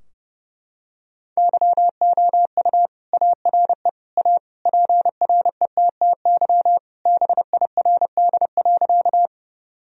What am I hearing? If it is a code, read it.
Morse code, heard as a keyed tone, about 30 wpm, 700 Hz. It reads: YOU ARE A PRETTY BIRD.
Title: Sentences from Top 1000 Words - Review Part 2 - 30wpm